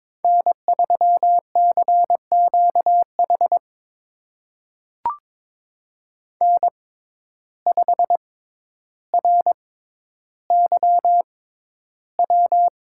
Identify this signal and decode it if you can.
Morse code, heard as a keyed tone, about 22 wpm, spaced as for 11 wpm, 700 Hz. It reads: N3CQH N5RYW
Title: USA Call Signs - 5 Characters - 6x Character Spacing - 22wpm